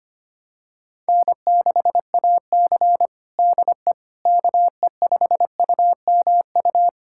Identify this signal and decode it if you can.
Morse code, heard as a keyed tone, about 25 wpm, 700 Hz. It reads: N6AC DE KE5UMU
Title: USA Callsign DE Callsign 25wpm